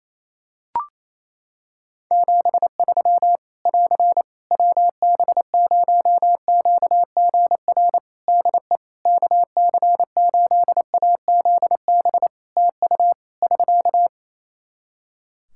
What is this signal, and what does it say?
Morse code, heard as a keyed tone, about 28 wpm, 700 Hz. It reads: 73 <AR> WB0QGR DE KC8AZB TU <SK>
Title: Practice QSO - Exchange 4 - Ending - 28wpm